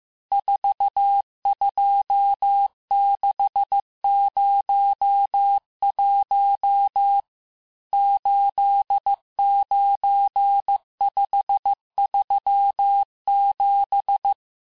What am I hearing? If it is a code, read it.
Morse code, heard as a keyed tone, about 15 wpm, 800 Hz. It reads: 42601 89537
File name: cours-du-lundi files / cours-1585-texte-chiffre-vitesse-14mpm